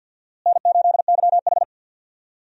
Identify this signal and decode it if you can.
Morse code, heard as a keyed tone, about 50 wpm, 700 Hz. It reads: N8YH
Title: USA Call Signs - 4 Characters - Repeated 2x - 50wpm